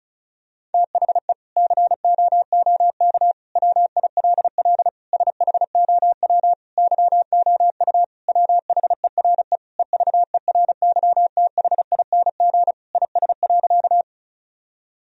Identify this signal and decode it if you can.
Morse code, heard as a keyed tone, about 36 wpm, 700 Hz. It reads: THE COOK WILL SHOW YOU WHERE EVERYTHING IS.
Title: Sentences from Top 1000 Words - Review Part 4 - 35wpm